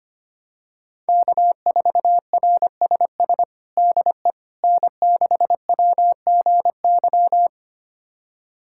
Morse code, heard as a keyed tone, about 25 wpm, 700 Hz. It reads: K4RSS DE N6WGY
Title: USA Callsign DE Callsign 25wpm